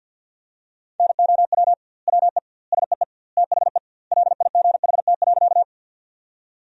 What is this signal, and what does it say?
Morse code, heard as a keyed tone, about 50 wpm, 700 Hz. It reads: NOW WE SEE THE LIGHT.
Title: 2x Word-Spacing - Sentences from Top 200 Words - 50wpm